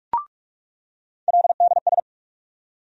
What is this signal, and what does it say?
Morse code, heard as a keyed tone, about 48 wpm, 700 Hz. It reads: LDS
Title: USA Call Sign Postfixes - 3 Letters 45wpm